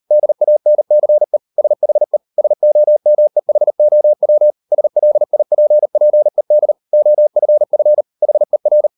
Morse code, heard as a keyed tone, about 38 wpm, 600 Hz. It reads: DANCE SHE SOMEHOW SLIPPED OFF HER